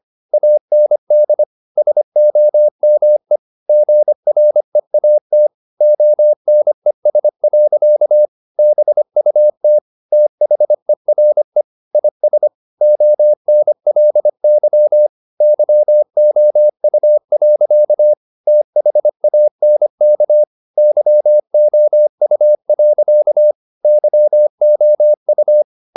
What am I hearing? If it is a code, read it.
Morse code, heard as a keyed tone, about 25 wpm, 600 Hz. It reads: AND SOME GREAT ONES. BUT THERE IS ONLY YOU. THANK YOU. YOU
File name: the old man and the sea part 1 25wpm